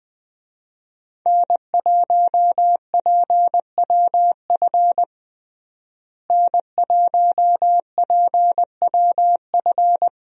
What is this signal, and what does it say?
Morse code, heard as a keyed tone, about 20 wpm, 700 Hz. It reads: N1PWF N1PWF
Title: USA Call Signs - 5 Characters - Repeated 3x - 20wpm